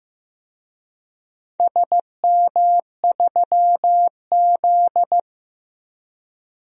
Morse code, heard as a keyed tone, about 15 wpm, 700 Hz. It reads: SM3Z
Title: DXCC-100 Call Signs - 4 Characters - Repeated 2x - 15wpm